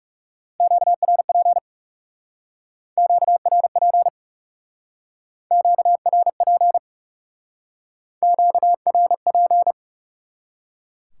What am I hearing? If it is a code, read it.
Morse code, heard as a keyed tone, about 39 wpm, 700 Hz. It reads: QRP QRP QRP QRP